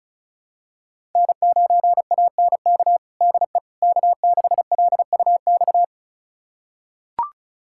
Morse code, heard as a keyed tone, about 36 wpm, 700 Hz. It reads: N9ANK DE K6LUX
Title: USA Callsign DE Callsign 35wpm